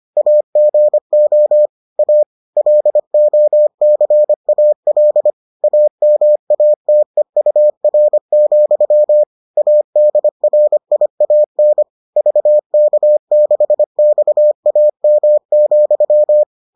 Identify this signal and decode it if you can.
Morse code, heard as a keyed tone, about 25 wpm, 600 Hz. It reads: AGO A LOCAL AMATEUR, ADRIAN VK6XAM,